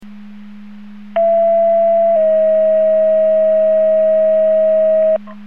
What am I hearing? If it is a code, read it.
Two-tone page, heard as a one-second tone, then a three-second tone, about 670 then 640 Hz. It is heard from 1.2 to 5.2 s.